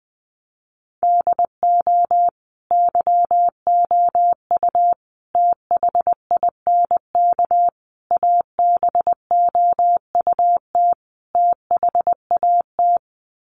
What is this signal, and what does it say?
Morse code, heard as a keyed tone, about 20 wpm, 700 Hz. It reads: DO YOU THINK ABOUT THAT
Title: Sets of 5 Words Encore2 20wpm